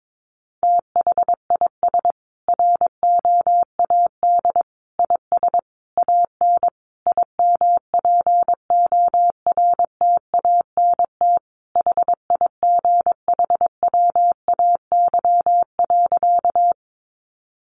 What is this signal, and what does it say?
Morse code, heard as a keyed tone, about 22 wpm, 700 Hz. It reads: THIS ROAD IS AN IMPORTANT HIGHWAY.
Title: Sentences from Top 1400 Words 22wpm